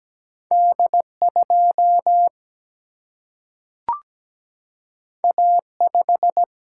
Morse code, heard as a keyed tone, about 17 wpm, 700 Hz. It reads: D 2 A 5
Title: Single-Letter Single-Number 17wpm